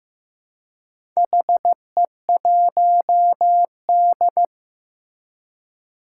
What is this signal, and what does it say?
Morse code, heard as a keyed tone, about 15 wpm, 700 Hz. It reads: HE1D
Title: Four Challenging Letter-Number Combinations 15wpm